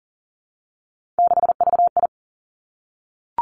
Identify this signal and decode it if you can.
Morse code, heard as a keyed tone, about 41 wpm, 700 Hz. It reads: 6VI
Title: Three Challenging Letter-Number Combinations 40wpm